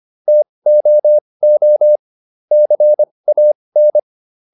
Morse code, heard as a keyed tone, about 25 wpm, spaced as for 20 wpm, 600 Hz. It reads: TOO CAN